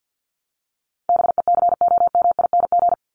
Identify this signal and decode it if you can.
Morse code, heard as a keyed tone, about 50 wpm, 700 Hz. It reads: BECOMING
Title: ING Suffix - Top 100 Words with Base Words 50wpm